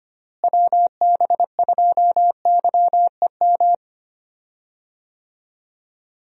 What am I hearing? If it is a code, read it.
Morse code, heard as a keyed tone, about 25 wpm, 700 Hz. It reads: WB2YEM
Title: USA Call Signs - 6 Characters - Repeated 3x - 25wpm